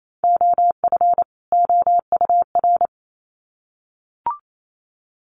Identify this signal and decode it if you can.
Morse code, heard as a keyed tone, about 28 wpm, 700 Hz. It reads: OF OUR
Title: Sets of 2 Words 28wpm